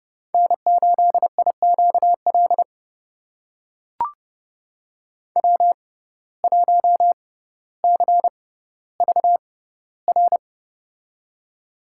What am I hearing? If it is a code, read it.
Morse code, heard as a keyed tone, about 30 wpm, spaced as for 15 wpm, 700 Hz. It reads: N8IQL W1CVR
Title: USA Call Signs - 5 Characters - 6x Character Spacing - 30wpm